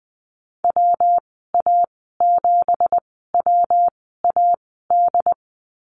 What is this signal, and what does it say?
Morse code, heard as a keyed tone, about 20 wpm, spaced as for 14 wpm, 700 Hz. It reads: WA7WAD
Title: 173 Lesson 28g - Number 9 - Call Signs 20wpm